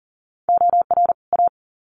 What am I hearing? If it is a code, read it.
Morse code, heard as a keyed tone, about 40 wpm, 700 Hz. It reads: OR A
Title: Sets of 2 Words 40wpm